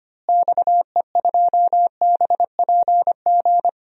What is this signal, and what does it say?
Morse code, heard as a keyed tone, about 25 wpm, 700 Hz. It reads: XE2BPG
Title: DXCC-100 Call Signs - 6 Characters - Repeated 3x - 25wpm